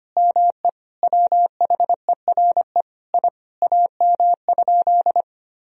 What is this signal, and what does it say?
Morse code, heard as a keyed tone, about 25 wpm, 700 Hz. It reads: ME WHERE I AM?
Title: Sentences from Top 1000 Words - Review Part 2 - 25wpm